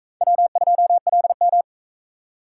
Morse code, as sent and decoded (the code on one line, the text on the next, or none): .-- ..--- .-.. --
W2LM